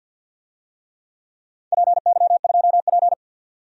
Morse code, heard as a keyed tone, about 52 wpm, 700 Hz. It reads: PY2P
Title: DXCC-100 Call Signs - 4 Characters - 50wpm